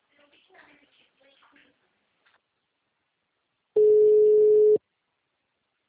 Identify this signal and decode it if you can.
Ringback tone (Europe, 425 Hz), with speech.